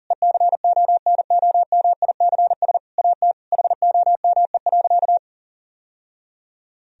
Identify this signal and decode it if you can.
Morse code, heard as a keyed tone, about 40 wpm, 700 Hz. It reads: ECONOMICS AT HOME.
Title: Sentences from Top 1000 Words - Review Part 3 - 40wpm